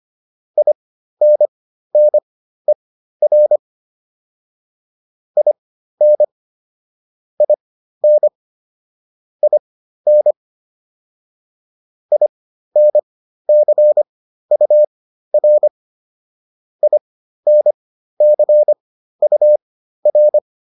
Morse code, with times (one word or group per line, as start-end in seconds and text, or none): 0.6-3.6 INNER
5.4-6.2 IN
7.4-8.3 IN
9.4-10.3 IN
12.1-15.7 INCUR
16.8-20.4 INCUR